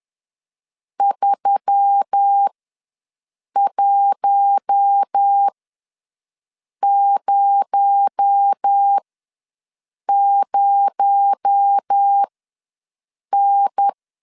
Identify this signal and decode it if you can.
Morse code, heard as a keyed tone, about 10 wpm, spaced as for 6 wpm, 800 Hz. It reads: 3100N
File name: LEZIONI / 6 PKX0 / 6 PKX0 ws 10wpm 4220